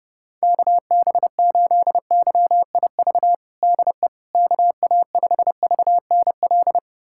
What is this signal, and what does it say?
Morse code, heard as a keyed tone, about 30 wpm, 700 Hz. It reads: KB8YIV DE KA5VNL
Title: USA Callsign DE Callsign - 6 Characters - 30wpm